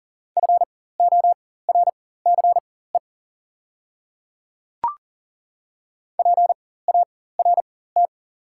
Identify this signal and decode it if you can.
Morse code, heard as a keyed tone, about 40 wpm, spaced as for 19 wpm, 700 Hz. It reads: FORCE PART
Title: 4x Character Spacing - Top 500 Words - 40wpm